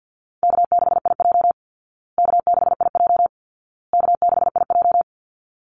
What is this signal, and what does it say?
Morse code, heard as a keyed tone, about 49 wpm, 700 Hz. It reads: K6IJ K6IJ K6IJ
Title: Easy Call Signs Repeated 3x - 50wpm